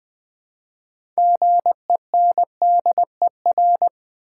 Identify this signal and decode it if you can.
Morse code, heard as a keyed tone, about 20 wpm, 700 Hz. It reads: GENDER
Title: Build A Word - 2 Letters At A Time - 6-Letter Words - 20wpm